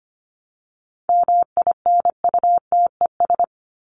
Morse code, heard as a keyed tone, about 25 wpm, 700 Hz. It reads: MINUTES
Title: ES:S Suffix - Top 100 Words with Base Words 25wpm